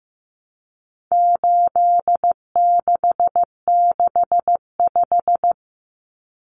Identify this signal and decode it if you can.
Morse code, heard as a keyed tone, about 15 wpm, 700 Hz. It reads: 8665
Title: Common Cut Numbers - 4-Digit 15wpm